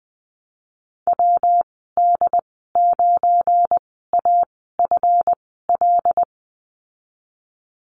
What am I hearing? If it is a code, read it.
Morse code, heard as a keyed tone, about 20 wpm, spaced as for 15 wpm, 700 Hz. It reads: WD9AFL
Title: 179 Lesson 29g - Slash - Call Signs 20wpm